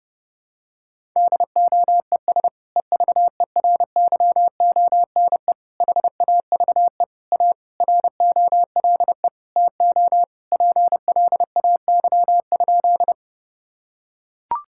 Morse code, heard as a keyed tone, about 32 wpm, 700 Hz. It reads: DOES EVERYONE HAVE A ROLE TO PLAY?
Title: Sentences from Top 1000 Words - Review Part 6 - 30wpm